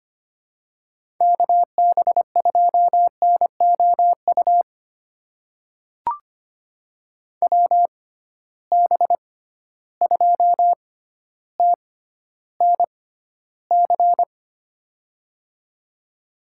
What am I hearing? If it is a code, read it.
Morse code, heard as a keyed tone, about 25 wpm, spaced as for 13 wpm, 700 Hz. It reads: KB2NOU WB2TNC
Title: USA Call Signs - 6 Characters - 6x Character Spacing - 25wpm